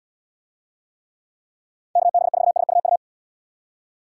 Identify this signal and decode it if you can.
Morse code, heard as a keyed tone, about 78 wpm, 690 Hz. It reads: KB4SUU